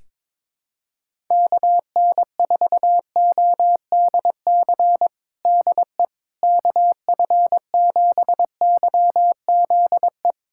Morse code, heard as a keyed tone, about 22 wpm, 700 Hz. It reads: KN4ODC DE KF7YZE
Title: USA Callsign DE Callsign 22wpm